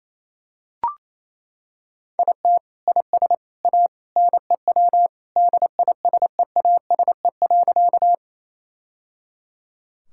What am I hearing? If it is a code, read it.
Morse code, heard as a keyed tone, about 28 wpm, 700 Hz. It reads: IT IS A NEW DISEASE.
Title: Sentences from Top 1000 Words - Review Part 6 - 28wpm